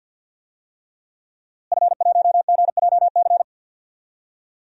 Morse code, heard as a keyed tone, about 51 wpm, 700 Hz. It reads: F1GJC
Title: DXCC-100 Call Signs - 5 Characters - Repeated 3x - 50wpm